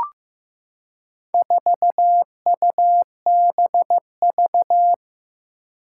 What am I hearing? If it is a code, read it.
Morse code, heard as a keyed tone, about 15 wpm, 700 Hz. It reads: 4UBV
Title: Four Challenging Letter-Number Combinations 15wpm